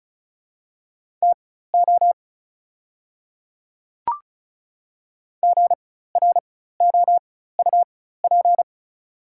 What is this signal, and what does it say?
Morse code, heard as a keyed tone, about 36 wpm, spaced as for 16 wpm, 690 Hz. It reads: TO GROUP
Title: 4x Character Spacing - Top 300 Words - 35wpm